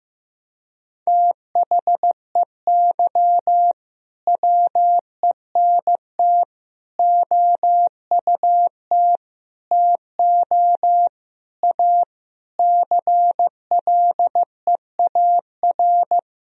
Morse code, heard as a keyed tone, about 15 wpm, 700 Hz. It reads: THEY WENT OUT TO A CLEAR
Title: Sentences from Top 1000 Words - Review Part 2 - 15wpm